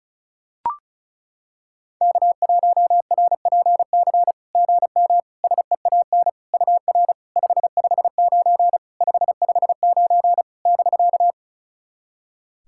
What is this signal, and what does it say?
Morse code, heard as a keyed tone, about 34 wpm, 690 Hz. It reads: K1RPC GM SEAN UR 559 559 <BK>
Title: SOTA QSO Practice 35wpm